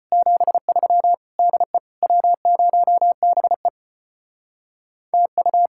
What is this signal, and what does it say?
Morse code, heard as a keyed tone, about 32 wpm, 700 Hz. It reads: 73 DE W0BE TU